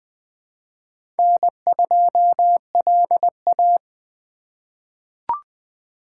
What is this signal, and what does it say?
Morse code, heard as a keyed tone, about 20 wpm, 700 Hz. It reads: N2LA
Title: USA Call Signs - 4 Characters - 2x Character Spacing - 20wpm